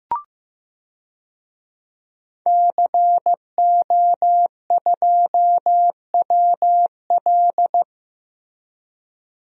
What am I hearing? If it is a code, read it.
Morse code, heard as a keyed tone, about 15 wpm, 700 Hz. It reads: CO2WL
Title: DXCC-100 Call Signs - 5 Characters - Repeated 3x - 15wpm